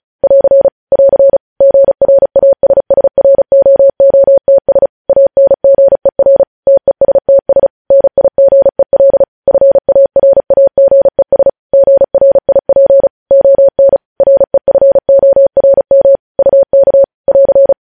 Morse code, heard as a keyed tone, about 36 wpm, 550 Hz. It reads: <AR> <AR> GRASSROOTS ANGER TESTS NIGEL FARAGES GRIP ON REFORM UK <AR>
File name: podcast / News headlines-1x-35x35